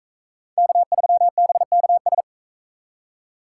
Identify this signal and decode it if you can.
Morse code, heard as a keyed tone, about 40 wpm, 700 Hz. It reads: K3BKS